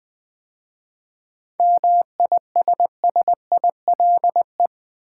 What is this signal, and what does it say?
Morse code, heard as a keyed tone, about 20 wpm, 690 Hz. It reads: MISSILE
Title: 7 Letter Words - First 3 Letters Given - 20wpm